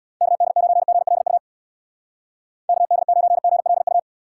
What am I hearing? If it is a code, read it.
Morse code, heard as a keyed tone, about 63 wpm, 700 Hz. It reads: KD9GLU KD9GLU